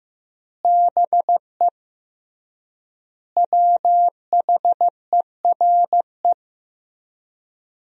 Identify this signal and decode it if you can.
Morse code, heard as a keyed tone, about 16 wpm, 700 Hz. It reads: BE WHERE
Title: 3x Word-Spacing - Sentences from Top 200 Words - 15wpm